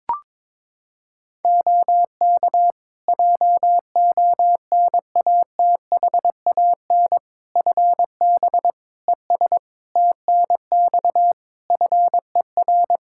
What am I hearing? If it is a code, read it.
Morse code, heard as a keyed tone, about 22 wpm, 700 Hz. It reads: OK JONATHAN FB ES TNX FER
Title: Practice QSO - Exchange 2 - Info - 22wpm